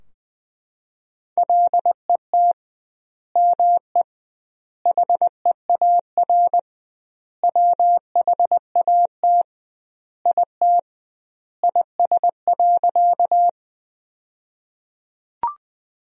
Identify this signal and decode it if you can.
Morse code, heard as a keyed tone, about 21 wpm, 700 Hz. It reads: LET ME HEAR WHAT IT IS.
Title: 2x Word-Spacing - Sentences from Top 300 Words - 20wpm